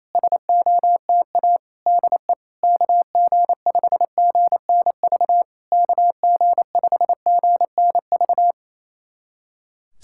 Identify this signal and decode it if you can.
Morse code, heard as a keyed tone, about 28 wpm, 700 Hz. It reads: SOTA DE KG5GNV KG5GNV
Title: SOTA QSO Practice 28wpm